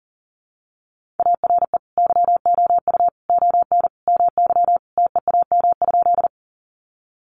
Morse code, heard as a keyed tone, about 40 wpm, 700 Hz. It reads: ARE YOU ON MY TEAM?
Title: Sentences from Top 1500 Words 40wpm